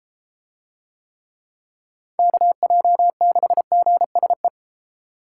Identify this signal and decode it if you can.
Morse code, heard as a keyed tone, about 32 wpm, 700 Hz. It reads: KJ6GSE